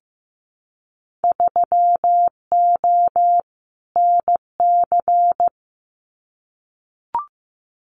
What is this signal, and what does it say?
Morse code, heard as a keyed tone, about 15 wpm, 700 Hz. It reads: 3O NC
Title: Winter Field Day Exchanges 15wpm